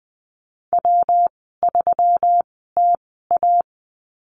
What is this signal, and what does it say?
Morse code, heard as a keyed tone, about 20 wpm, spaced as for 15 wpm, 700 Hz. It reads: W3TA
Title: 173 Lesson 28g - Number 9 - Call Signs 20wpm